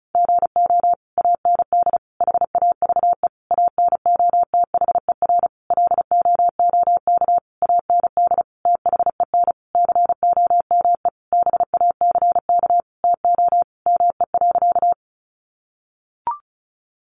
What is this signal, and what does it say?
Morse code, heard as a keyed tone, about 34 wpm, 700 Hz. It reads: GO AND HAVE ANOTHER LOOK AND THEN COME BACK TO ME.
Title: Sentences from Top 600 Words 35wpm